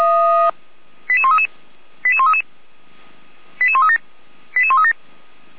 Five-tone selective call (ZVEI), heard as quick runs of five tones, tones of about 70 ms. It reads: 88133 88133 88137 88137
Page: /455/SDR/ZVEI/88341/